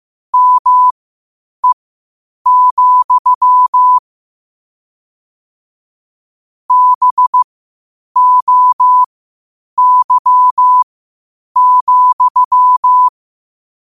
Morse code, heard as a keyed tone, about 15 wpm, spaced as for 8 wpm, 1000 Hz. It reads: ME, BOY,
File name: qotdfiles / 8wpm